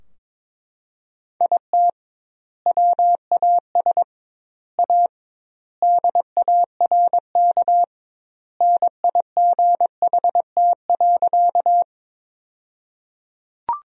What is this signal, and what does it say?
Morse code, heard as a keyed tone, about 22 wpm, 700 Hz. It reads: IT WAS A DARK NIGHT.
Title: 2x Word-Spacing - Sentences from Top 500 Words - 22wpm